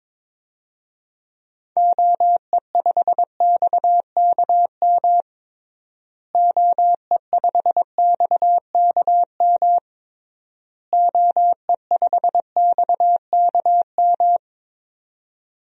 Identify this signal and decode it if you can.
Morse code, heard as a keyed tone, about 22 wpm, 700 Hz. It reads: OE5XKM OE5XKM OE5XKM
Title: DXCC-100 Call Signs - 6 Characters - Repeated 3x - 22wpm